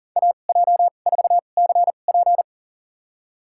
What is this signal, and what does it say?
Morse code, heard as a keyed tone, about 40 wpm, spaced as for 29 wpm, 700 Hz. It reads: AJ4CP
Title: USA Call Signs - 5 Characters - 2x Character Spacing - 40wpm